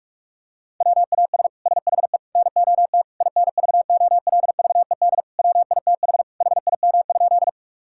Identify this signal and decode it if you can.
Morse code, heard as a keyed tone, about 46 wpm, 700 Hz. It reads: WAS SHE NOT INVOLVED WITH HIM?
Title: Sentences from Top 1000 Words - Review Part 5 - 45wpm